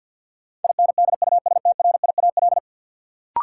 Morse code, heard as a keyed tone, about 53 wpm, 700 Hz. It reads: INDUSTRIAL
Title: Sets of 5 Words - New Word Drill - 50wpm